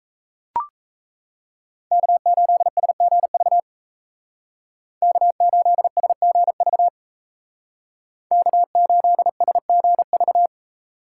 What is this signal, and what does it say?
Morse code, heard as a keyed tone, about 33 wpm, 700 Hz. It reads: K8SGV K8SGV K8SGV
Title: Speed Racing - USA 1x3 Call Signs 28wpm